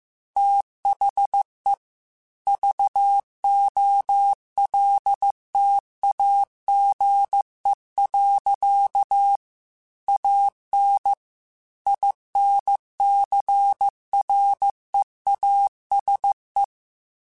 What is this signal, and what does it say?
Morse code, heard as a keyed tone, about 15 wpm, 800 Hz. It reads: THE VOLTAGE. AN INCREASE